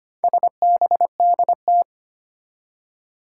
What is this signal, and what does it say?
Morse code, heard as a keyed tone, about 25 wpm, 700 Hz. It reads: SBDT
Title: Four Challenging Letter-Number Combinations 25wpm